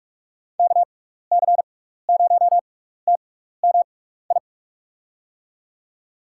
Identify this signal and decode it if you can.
Morse code, heard as a keyed tone, about 44 wpm, spaced as for 15 wpm, 700 Hz. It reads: KC0TMI
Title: USA Call Signs - 6 Characters - 6x Character Spacing - 45wpm